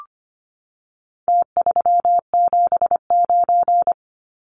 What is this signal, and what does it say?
Morse code, heard as a keyed tone, about 25 wpm, 700 Hz. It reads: T379
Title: Numbers - Contest Style - 4-Digit 25wpm